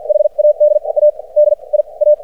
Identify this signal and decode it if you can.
Morse code, heard as a keyed tone, about 53 wpm, spaced as for 36 wpm, 600 Hz. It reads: HAD A DIA
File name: archived / CRSnet